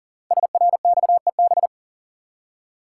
Morse code, heard as a keyed tone, about 40 wpm, 700 Hz. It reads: SRXEB
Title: Code Groups - 5 Letters - 40wpm